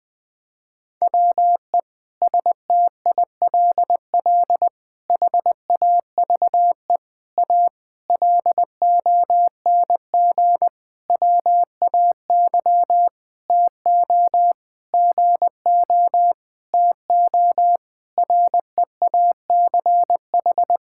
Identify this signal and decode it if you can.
Morse code, heard as a keyed tone, about 20 wpm, 700 Hz. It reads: WE STILL HAVE A LONG WAY TO GO TO REACH
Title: Sentences with 7-Word Phrases - Spoken Beginning - 20wpm